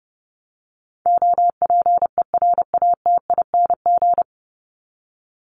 Morse code, heard as a keyed tone, about 30 wpm, 700 Hz. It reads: OPERATING
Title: ING Suffix - Top 100 Words with Base Words 30wpm